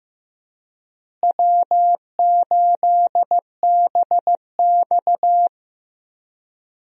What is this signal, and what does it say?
Morse code, heard as a keyed tone, about 15 wpm, 700 Hz. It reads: W8BX
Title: USA Call Signs - 4 Characters - Repeated 3x - 15wpm